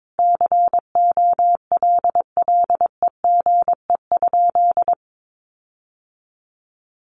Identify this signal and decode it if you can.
Morse code, heard as a keyed tone, about 22 wpm, 700 Hz. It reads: COLLEGE?
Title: Sentences from Top 1300 Words 22wpm